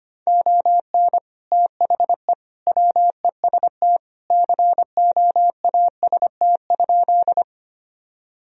Morse code, heard as a keyed tone, about 25 wpm, 700 Hz. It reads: ON THE WEST COAST?
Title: Sentences from Top 1000 Words - Review Part 4 - 25wpm